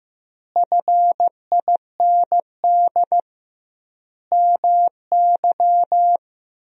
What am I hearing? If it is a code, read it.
Morse code, heard as a keyed tone, about 15 wpm, 700 Hz. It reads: FIND MY
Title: 2x Word-Spacing - Sentences from Top 200 Words - 15wpm